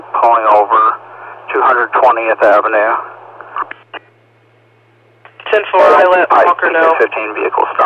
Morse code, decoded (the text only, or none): none